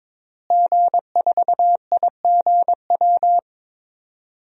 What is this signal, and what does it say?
Morse code, heard as a keyed tone, about 22 wpm, 700 Hz. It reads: G4IGW
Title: DXCC-100 Call Signs - 5 Characters - Repeated 2x - 22wpm